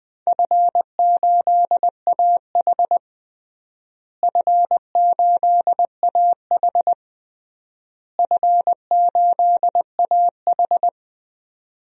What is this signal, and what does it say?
Morse code, heard as a keyed tone, about 20 wpm, 700 Hz. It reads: F8AH F8AH F8AH
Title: DXCC-100 Call Signs - 4 Characters - Repeated 3x - 20wpm